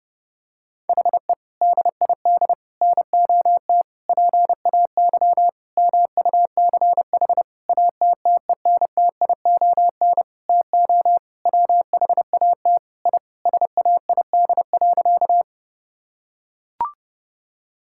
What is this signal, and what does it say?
Morse code, heard as a keyed tone, about 30 wpm, 700 Hz. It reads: HE DID NOT PAY MUCH ATTENTION TO WHAT I SAID.
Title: Sentences from Top 1000 Words - Review Part 6 - 30wpm